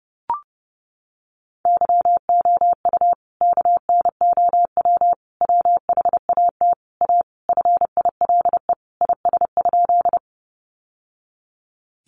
Morse code, heard as a keyed tone, about 30 wpm, 700 Hz. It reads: YOU KNOW WHAT A FILE IS?
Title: Sentences from Top 1200 Words 30wpm